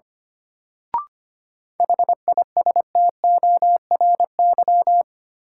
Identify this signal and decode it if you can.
Morse code, heard as a keyed tone, about 25 wpm, 700 Hz. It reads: HISTORY
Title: Sentences from Top 600 Words - New Word Drill - 25wpm